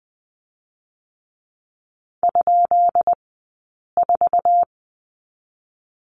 Morse code, heard as a keyed tone, about 20 wpm, 700 Hz. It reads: ? 4
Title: 137 Lesson 23d - Letter f - Two Characters 20wpm